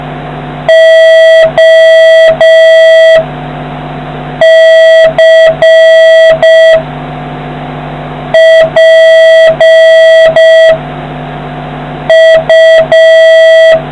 Morse code, heard as a keyed tone, about 5 wpm, 1950 Hz, with noise. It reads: O C P U